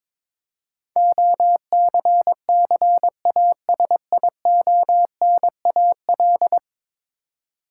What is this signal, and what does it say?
Morse code, heard as a keyed tone, about 22 wpm, 700 Hz. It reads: OCCASIONAL
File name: practice / 22wpm / 10 Letter Words - First 3 Letters Given - 22wpm